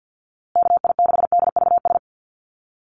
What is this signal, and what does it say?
Morse code, heard as a keyed tone, about 50 wpm, 700 Hz. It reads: KI6DVS
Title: USA Call Signs Repeated 2x 50wpm